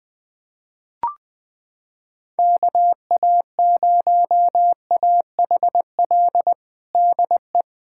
Morse code, heard as a keyed tone, about 20 wpm, 700 Hz. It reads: KA0AHL DE
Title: USA Callsign DE Callsign 20wpm